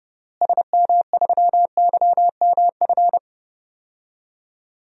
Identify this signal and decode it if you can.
Morse code, heard as a keyed tone, about 30 wpm, 700 Hz. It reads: SM3YMF
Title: DXCC-100 Call Signs - 6 Characters - Repeated 2x - 30wpm